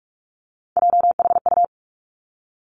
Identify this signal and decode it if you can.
Morse code, heard as a keyed tone, about 46 wpm, 700 Hz. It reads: JHU
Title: Three Challenging Letter-Number Combinations 45wpm